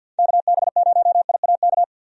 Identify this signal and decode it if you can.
Morse code, heard as a keyed tone, about 50 wpm, 700 Hz. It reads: KB0IAK
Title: USA Call Signs - 6 Characters - 50wpm